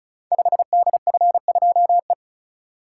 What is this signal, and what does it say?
Morse code, heard as a keyed tone, about 36 wpm, 690 Hz. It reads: 5DF2E
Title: Code Groups - 5 Letter:Numbers - 35wpm